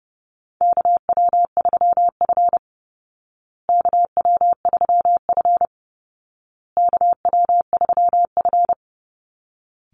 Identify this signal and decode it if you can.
Morse code, heard as a keyed tone, about 28 wpm, 700 Hz. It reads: KW3F KW3F KW3F
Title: Call Signs Repeated 2x - 30wpm